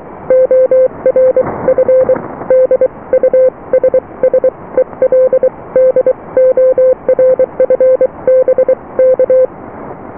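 Morse code, heard as a keyed tone, about 23 wpm, spaced as for 20 wpm, 520 Hz. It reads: ORF DUSSELDORF BK
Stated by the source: Morse QSO on 14 MHz with a German station, received at RST 579.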